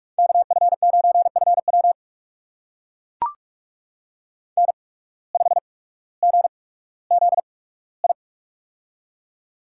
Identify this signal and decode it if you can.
Morse code, heard as a keyed tone, about 44 wpm, spaced as for 19 wpm, 690 Hz. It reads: KF9FW N5GZI
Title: USA Call Signs - 5 Characters - 8x Character Spacing - 45wpm